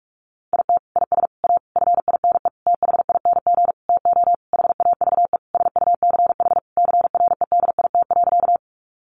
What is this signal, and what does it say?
Morse code, heard as a keyed tone, about 44 wpm, 700 Hz. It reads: IT IS A FINE THING TO HAVE SUCH CREDIT.
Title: Sentences from Top 1000 Words 45wpm